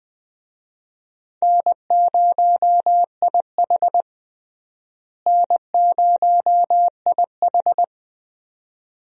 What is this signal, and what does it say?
Morse code, heard as a keyed tone, about 20 wpm, 700 Hz. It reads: N0IH N0IH
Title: USA Call Signs - 4 Characters - Repeated 2x - 20wpm